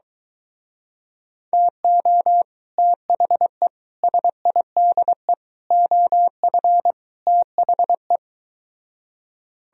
Morse code, heard as a keyed tone, about 23 wpm, 700 Hz. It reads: TO THE SIDE OF THE